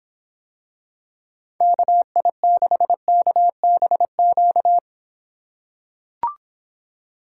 Morse code, heard as a keyed tone, about 26 wpm, 700 Hz. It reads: KI6KBQ